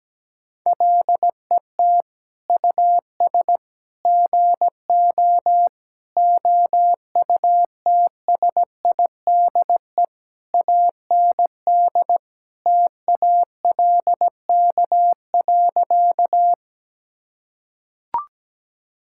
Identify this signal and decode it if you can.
Morse code, heard as a keyed tone, about 17 wpm, 700 Hz. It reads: LET US GO OUTSIDE AND TALK.
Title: Sentences from Top 1000 Words - Review Part 6 - 17wpm